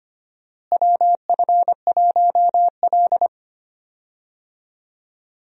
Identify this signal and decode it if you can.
Morse code, heard as a keyed tone, about 25 wpm, 700 Hz. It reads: WF1L
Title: USA Call Signs - 4 Characters - Repeated 3x - 25wpm